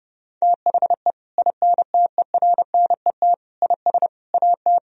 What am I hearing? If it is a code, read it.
Morse code, heard as a keyed tone, about 32 wpm, 700 Hz. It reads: THE INTERNET IS AT
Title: Sentences from Top 1000 Words - Review Part 4 - 30wpm